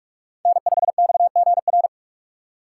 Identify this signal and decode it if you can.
Morse code, heard as a keyed tone, about 46 wpm, 700 Hz. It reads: N5XGR
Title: USA Call Signs - 6x Character Spacing - 45wpm